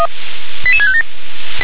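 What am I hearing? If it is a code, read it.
Five-tone selective call (ZVEI), heard as quick runs of five tones, tones of about 70 ms. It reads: 88657
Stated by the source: Five-tone ZVEI selective call received on SDR.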